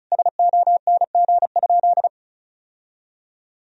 Morse code, heard as a keyed tone, about 34 wpm, 700 Hz. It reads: SONG?
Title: Sentences from Top 1000 Words - Review Part 2 - 35wpm